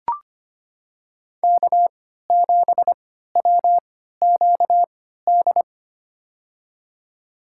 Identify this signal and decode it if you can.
Morse code, heard as a keyed tone, about 25 wpm, spaced as for 14 wpm, 700 Hz. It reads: K7WQD
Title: USA Call Signs - 5 Characters - 3x Character Spacing - 25wpm